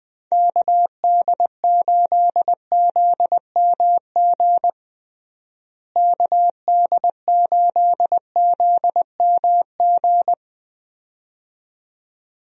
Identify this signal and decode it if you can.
Morse code, heard as a keyed tone, about 20 wpm, 700 Hz. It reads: KD8ZMG KD8ZMG
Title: USA Call Signs - 6 Characters - Repeated 2x - 20wpm